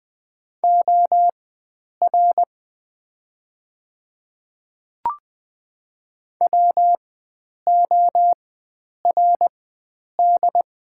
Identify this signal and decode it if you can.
Morse code, heard as a keyed tone, about 20 wpm, spaced as for 9 wpm, 690 Hz. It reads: OR WORD
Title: 4x Character Spacing - Top 300 Words - 20wpm